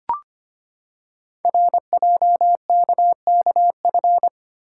Morse code, heard as a keyed tone, about 25 wpm, 700 Hz. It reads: RJKKF
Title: Code Groups - 5 Letters - 25wpm